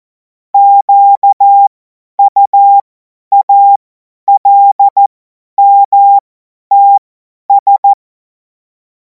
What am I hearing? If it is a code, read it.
Morse code, heard as a keyed tone, about 14 wpm, spaced as for 10 wpm, 800 Hz. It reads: QUALMTS
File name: Morsekurs / Witze / Witz 18. August 2026 50 F70